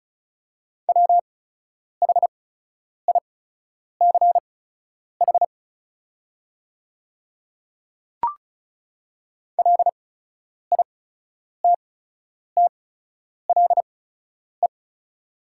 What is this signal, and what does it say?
Morse code, heard as a keyed tone, about 36 wpm, spaced as for 10 wpm, 690 Hz. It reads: WHICH LITTLE
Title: 8x Character Spacing - Top 200 Words - 35wpm